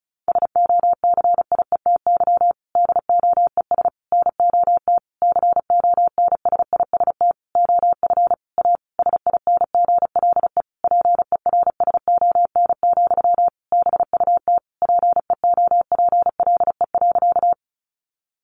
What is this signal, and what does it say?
Morse code, heard as a keyed tone, about 34 wpm, 700 Hz. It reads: SOCIETY DOES NOT CONSIST OF A SINGLE PERSON, BUT PEOPLE.
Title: Sentences from Top 1400 Words 35wpm